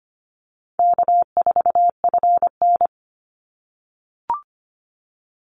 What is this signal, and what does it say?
Morse code, heard as a keyed tone, about 25 wpm, 700 Hz. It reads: K4FN
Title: Call Signs - 25wpm